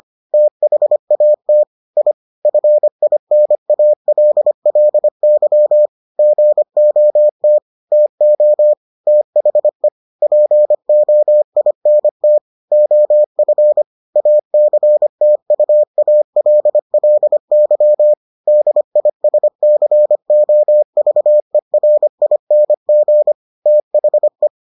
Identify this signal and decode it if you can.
Morse code, heard as a keyed tone, about 25 wpm, 600 Hz. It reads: THAT I FINALLY GOT TO THE POINT OF ACTUALLY DISCOVERING THE